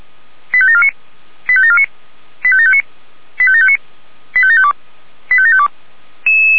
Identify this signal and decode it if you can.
Five-tone selective call (ZVEI), heard as quick runs of five tones, tones of about 70 ms. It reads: 85749 85749 85759 85759 85752 85752
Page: /464/SDR/ZVEI/85863/